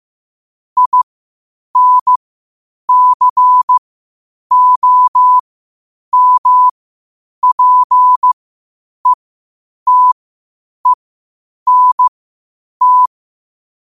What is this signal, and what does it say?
Morse code, heard as a keyed tone, about 15 wpm, spaced as for 8 wpm, 1000 Hz. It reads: INCOMPETENT
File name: qotdfiles / 8wpm